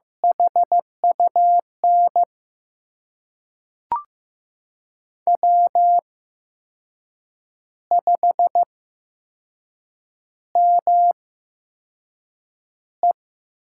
Morse code, heard as a keyed tone, about 15 wpm, spaced as for 6 wpm, 700 Hz. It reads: HUN W5ME
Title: USA Call Signs - 8x Character Spacing - 15wpm